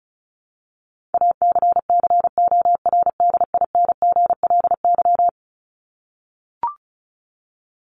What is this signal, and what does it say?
Morse code, heard as a keyed tone, about 34 wpm, 700 Hz. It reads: ACCORDINGLY
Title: LY Suffix - Top 100 Words with Base Words 35wpm